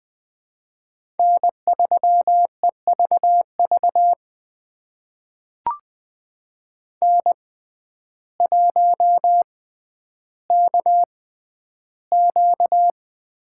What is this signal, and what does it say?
Morse code, heard as a keyed tone, about 20 wpm, spaced as for 11 wpm, 700 Hz. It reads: N3EVV N1KQ
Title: USA Call Signs - 5 Characters - 6x Character Spacing - 20wpm